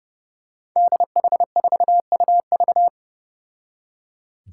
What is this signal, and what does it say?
Morse code, heard as a keyed tone, about 30 wpm, 700 Hz. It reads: DH4UV
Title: Five Challenging Letter-Number Combinations 30wpm